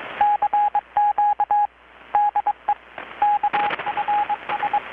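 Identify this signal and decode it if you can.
Morse code, heard as a keyed tone, about 21 wpm, 830 Hz, with radio noise. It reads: CQ DE KFS
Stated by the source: Morse from KPH's Night of Nights XX maritime radiotelegraphy event, received in Chiapas, Mexico about 3,800 km away on a Kenwood R-600 with a grounded 10 m long-wire antenna.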